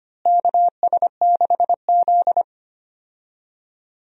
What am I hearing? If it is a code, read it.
Morse code, heard as a keyed tone, about 25 wpm, 700 Hz. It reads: KS6Z
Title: USA Call Signs - 4 Characters - Repeated 2x - 25wpm